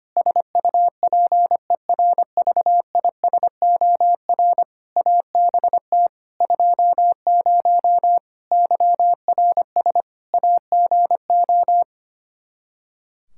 Morse code, heard as a keyed tone, about 25 wpm, 700 Hz. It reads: SUPERVISOR ABT 20 YRS AGO
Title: Practice QSO - Exchange 3 - More Info - 25wpm